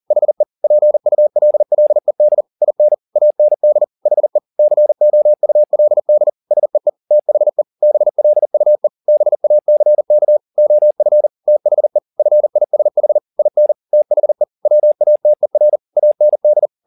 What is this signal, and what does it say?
Morse code, heard as a keyed tone, about 40 wpm, 600 Hz. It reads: HE PULLED IN AND HE COULD SEE THE BLUE BACK OF THE FISH IN THE WATER AND